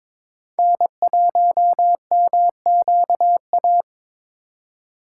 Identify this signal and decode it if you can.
Morse code, heard as a keyed tone, about 26 wpm, 700 Hz. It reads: N1MQA
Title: Speed Racing - USA 1x3 Call Signs 22wpm